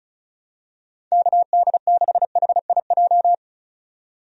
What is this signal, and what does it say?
Morse code, heard as a keyed tone, about 36 wpm, 700 Hz. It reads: KD6HIJ
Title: USA Call Signs - 6 Characters - 35wpm